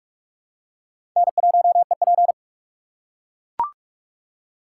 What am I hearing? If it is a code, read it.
Morse code, heard as a keyed tone, about 40 wpm, 700 Hz. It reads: N1EP